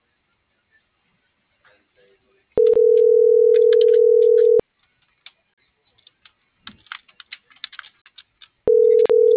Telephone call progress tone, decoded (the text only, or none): ringback tone